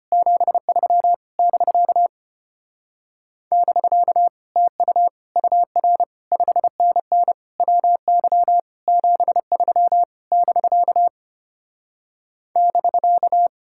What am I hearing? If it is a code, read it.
Morse code, heard as a keyed tone, about 33 wpm, 700 Hz. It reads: 73 <BK> <BK> TU UR 5NN WY 73 <BK> <BK>